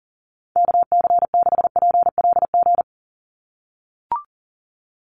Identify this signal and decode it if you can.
Morse code, heard as a keyed tone, about 40 wpm, 700 Hz. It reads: KC6PLG
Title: USA Call Signs 40wpm